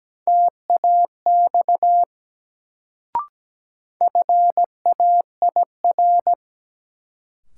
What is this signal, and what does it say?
Morse code, heard as a keyed tone, about 17 wpm, 700 Hz. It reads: TAX FAIR
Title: Sentences from Top 1000 Words - Review Part 5 - New Word Drill - 17wpm